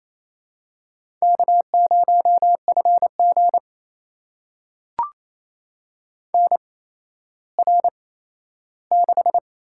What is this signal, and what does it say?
Morse code, heard as a keyed tone, about 28 wpm, 700 Hz. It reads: K 0 F G N R 6
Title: USA Call Signs - 4 Characters - 8x Character Spacing - 28wpm